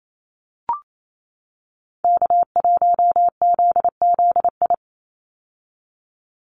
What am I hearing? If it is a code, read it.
Morse code, heard as a keyed tone, about 28 wpm, 700 Hz. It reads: K1ZZI
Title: Call Signs - 28wpm